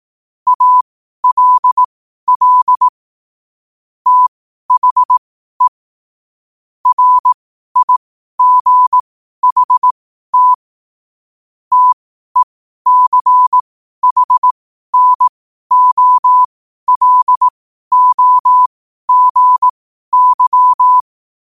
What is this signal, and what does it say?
Morse code, heard as a keyed tone, about 18 wpm, spaced as for 12 wpm, 1000 Hz. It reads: ALL THE RIGHT TECHNOLOGY
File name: qotdfiles / 12wpm